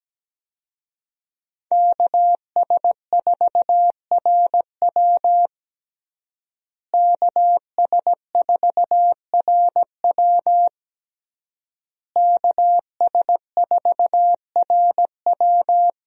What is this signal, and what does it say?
Morse code, heard as a keyed tone, about 17 wpm, 700 Hz. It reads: KS4RW KS4RW KS4RW
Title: USA Call Signs - 5 Characters - Repeated 3x - 17wpm